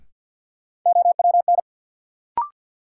Morse code, heard as a keyed tone, about 52 wpm, 700 Hz. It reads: OWN
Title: Sets of 2 Words - New Word Drill - 50wpm